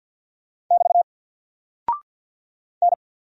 Morse code, heard as a keyed tone, about 48 wpm, 700 Hz. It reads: <BT> N
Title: Single Character (Letter:Number:Punctuation:Prosign) 50wpm